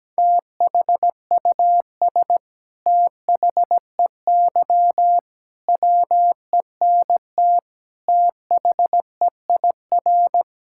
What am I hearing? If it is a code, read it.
Morse code, heard as a keyed tone, about 17 wpm, 700 Hz. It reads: THUS THEY WENT THEIR
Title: Sentences from Top 1000 Words - Review Part 5 - 17wpm